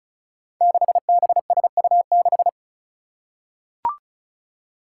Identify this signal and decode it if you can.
Morse code, heard as a keyed tone, about 36 wpm, 700 Hz. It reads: 6BSU6
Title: Five Challenging Letter-Number Combinations 35wpm